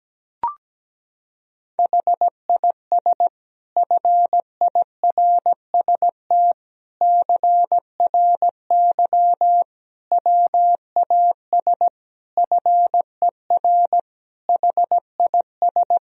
Morse code, heard as a keyed tone, about 17 wpm, 700 Hz. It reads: HIS FIRST CRY WAS FER HIS
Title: Sentences from Top 500 Words and Abbreviations 17wpm